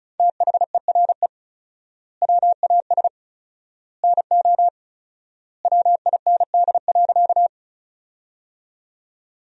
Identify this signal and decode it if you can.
Morse code, heard as a keyed tone, about 34 wpm, 700 Hz. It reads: THERE WAS NO WIND.
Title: 4x Word-Spacing - Sentences from Top 400 Words - 35wpm